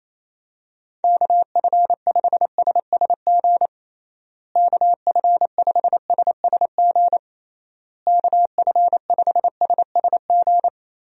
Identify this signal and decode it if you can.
Morse code, heard as a keyed tone, about 28 wpm, 700 Hz. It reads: KF5SSG KF5SSG KF5SSG
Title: USA Call Signs - 6 Characters - Repeated 3x - 28wpm